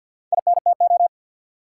Morse code, heard as a keyed tone, about 53 wpm, 700 Hz. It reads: INTO
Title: Four Letter Words - 50wpm